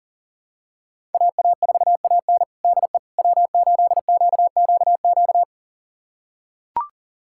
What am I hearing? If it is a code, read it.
Morse code, heard as a keyed tone, about 40 wpm, 700 Hz. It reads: AA4AN DE W8QQQ
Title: USA Callsign DE Callsign 40wpm